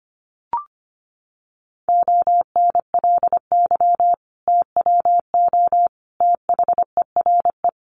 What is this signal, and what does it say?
Morse code, heard as a keyed tone, about 25 wpm, 700 Hz. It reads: ONLY TWO THERE
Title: Sentences from Top 200 Words 25wpm